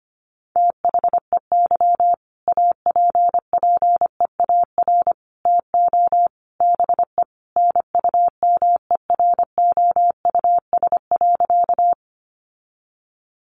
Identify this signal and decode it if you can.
Morse code, heard as a keyed tone, about 25 wpm, 700 Hz. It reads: THEY APPEAR TO BE NUMEROUS.
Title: Sentences from Top 1400 Words 25wpm